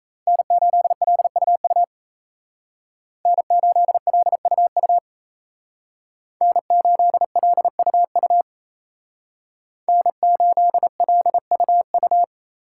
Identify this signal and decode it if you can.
Morse code, heard as a keyed tone, about 36 wpm, 700 Hz. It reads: N8LUU N8LUU N8LUU N8LUU